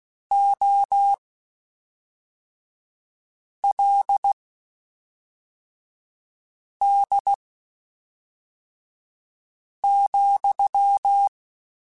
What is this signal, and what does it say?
Morse code, heard as a keyed tone, about 16 wpm, spaced as for 3 wpm, 800 Hz. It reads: OLD,
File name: (spaced) / QSOs Spaced-mp3s / kc7haa @5 sp